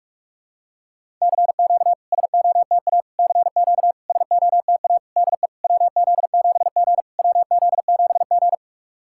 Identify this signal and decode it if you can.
Morse code, heard as a keyed tone, about 42 wpm, 690 Hz. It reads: CQ SOTA CQ SOTA DE WZ7G WZ7G
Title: SOTA QSO Practice 45wpm